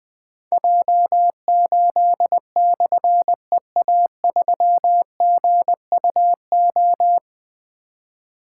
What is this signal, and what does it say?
Morse code, heard as a keyed tone, about 21 wpm, 700 Hz. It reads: J8/EA3GUO
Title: Difficult Call Signs 20wpm